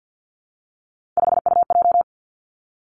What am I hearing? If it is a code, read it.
Morse code, heard as a keyed tone, about 50 wpm, 700 Hz. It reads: 5UJ
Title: Three Challenging Letter-Number Combinations 50wpm